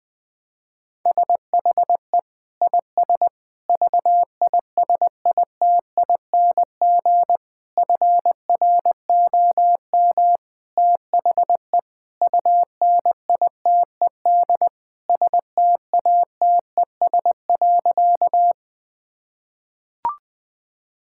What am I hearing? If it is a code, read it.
Morse code, heard as a keyed tone, about 20 wpm, 700 Hz. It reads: SHE IS VISITING FROM THE UNITED STATES.
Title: Sentences with 4-Word Phrases - 20wpm